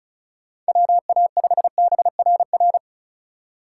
Morse code, heard as a keyed tone, about 36 wpm, 700 Hz. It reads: WA5BRR
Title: USA Call Signs - 6 Characters - 3x Character Spacing - 35wpm